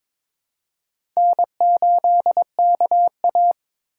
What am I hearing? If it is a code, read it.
Morse code, heard as a keyed tone, about 22 wpm, 700 Hz. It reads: N8KA
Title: USA Call Signs - 4 Characters - 4x Character Spacing - 22wpm